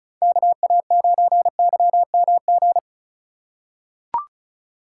Morse code, heard as a keyed tone, about 36 wpm, 700 Hz. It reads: KA9YMG
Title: K1USN SST QSO Practice - 35wpm